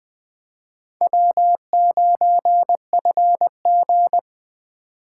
Morse code, heard as a keyed tone, about 20 wpm, 700 Hz. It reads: W9FG
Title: Speed Racing - USA 1x2 Call Signs 17wpm